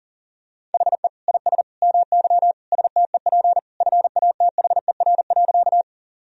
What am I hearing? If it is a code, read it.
Morse code, heard as a keyed tone, about 40 wpm, 700 Hz. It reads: HE IS MY STEP FATHER.
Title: Sentences from Top 1000 Words - Review Part 2 - 40wpm